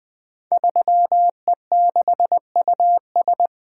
Morse code, heard as a keyed tone, about 20 wpm, 700 Hz. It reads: 3E6US
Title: Code Groups - 5 Letter:Numbers - 20wpm